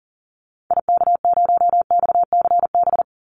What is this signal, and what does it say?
Morse code, heard as a keyed tone, about 40 wpm, 700 Hz. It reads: IK0XCB
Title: Call Signs - 40wpm